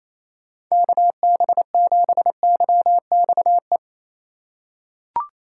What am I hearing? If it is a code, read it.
Morse code, heard as a keyed tone, about 28 wpm, 700 Hz. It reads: KB7YXE
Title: USA Call Signs - 6x Character Spacing - 28wpm